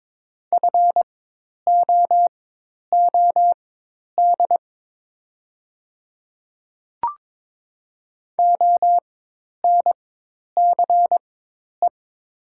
Morse code, heard as a keyed tone, about 22 wpm, spaced as for 10 wpm, 700 Hz. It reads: FOOD ONCE
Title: 4x Character Spacing - Top 300 Words - 22wpm